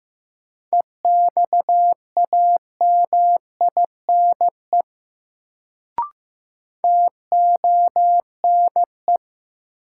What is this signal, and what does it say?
Morse code, heard as a keyed tone, about 15 wpm, 700 Hz. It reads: EXAMINE TONE
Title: Sentences from Top 1600 Words - New Word Drill - 15wpm